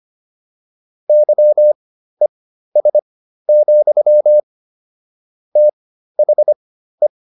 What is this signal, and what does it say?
Morse code, heard as a keyed tone, about 25 wpm, spaced as for 13 wpm, 600 Hz. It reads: YES, THE